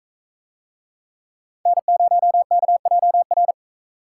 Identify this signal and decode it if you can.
Morse code, heard as a keyed tone, about 43 wpm, 700 Hz. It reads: N0KJR